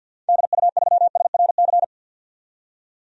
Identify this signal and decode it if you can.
Morse code, heard as a keyed tone, about 52 wpm, 700 Hz. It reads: DU3SRC
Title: DXCC-100 Call Signs - 6 Characters - 50wpm